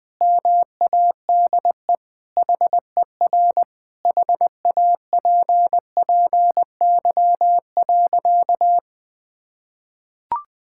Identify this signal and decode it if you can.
Morse code, heard as a keyed tone, about 20 wpm, 700 Hz. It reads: MADE HER HAPPY.
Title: Sentences with 4-Word Phrases - 20wpm